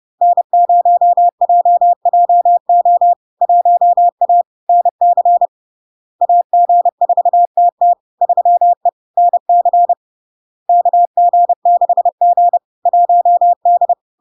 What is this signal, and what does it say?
Morse code, heard as a keyed tone, about 30 wpm, 700 Hz. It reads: N0JJO 1A NC AG4TT 3E NC KG6G 1D
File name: LARK-FD-2022-Log-in-CW-30wpm-700-Hz